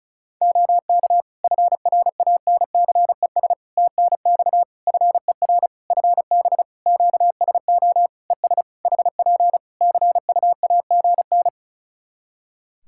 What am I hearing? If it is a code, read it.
Morse code, heard as a keyed tone, about 36 wpm, 700 Hz. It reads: OK FRANCES TNX FER FB QSO ES HP CUAGN
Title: Practice QSO - Exchange 4 - Ending - 35wpm